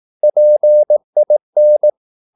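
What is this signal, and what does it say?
Morse code, heard as a keyed tone, about 18 wpm, 600 Hz. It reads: PIN
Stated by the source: Morse practice at 18 wpm, English text, from CW med Gnister.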